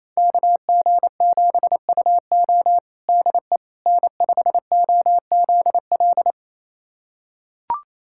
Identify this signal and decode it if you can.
Morse code, heard as a keyed tone, about 28 wpm, 700 Hz. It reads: KG7UO DE N5OZL
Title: USA Callsign DE Callsign - 5 Characters - 28wpm